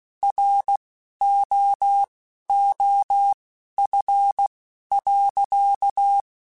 Morse code, heard as a keyed tone, about 16 wpm, spaced as for 11 wpm, 800 Hz. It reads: ROOF.
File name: Weather Sets / Weather Sets-mp3s / Weather 4 24 @10